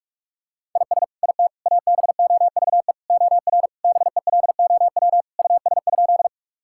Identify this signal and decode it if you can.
Morse code, heard as a keyed tone, about 46 wpm, 700 Hz. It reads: IS IT ABOVE OR BELOW US?
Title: Sentences from Top 1000 Words - Review Part 5 - 45wpm